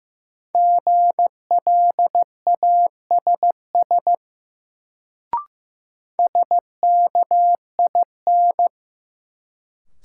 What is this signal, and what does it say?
Morse code, heard as a keyed tone, about 15 wpm, 700 Hz. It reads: GLASS SKIN
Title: Sentences from Top 1300 Words - New Word Drill Alt - 15wpm